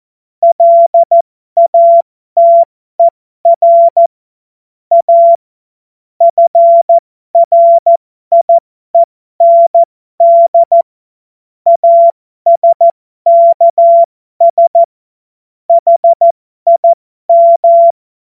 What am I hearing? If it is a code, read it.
Morse code, heard as a keyed tone, about 14 wpm, spaced as for 12 wpm, 680 Hz. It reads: LATER A FRIEND ASKS HIM
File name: Morsekurs / Witze